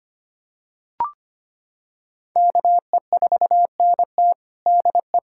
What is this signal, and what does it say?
Morse code, heard as a keyed tone, about 25 wpm, 700 Hz. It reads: KE4NT DE
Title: USA Callsign DE Callsign - 5 Characters - 25wpm